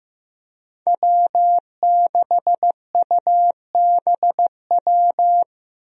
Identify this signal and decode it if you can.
Morse code, heard as a keyed tone, about 15 wpm, 700 Hz. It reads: W6UBW
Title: USA Call Signs - 5 Characters - Repeated 3x - 15wpm